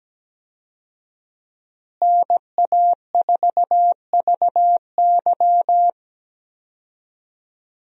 Morse code, heard as a keyed tone, about 17 wpm, 700 Hz. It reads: NA4VY
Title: USA Call Signs - 5 Characters - Repeated 2x - 17wpm